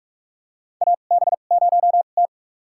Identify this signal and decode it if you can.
Morse code, heard as a keyed tone, about 46 wpm, spaced as for 33 wpm, 700 Hz. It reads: AB0T
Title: USA Call Signs - 4 Characters - 2x Character Spacing - 45wpm